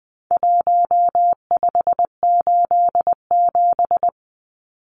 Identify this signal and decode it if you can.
Morse code, heard as a keyed tone, about 20 wpm, 700 Hz. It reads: 1587
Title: Numbers - Contest Style - 4-Digit 20wpm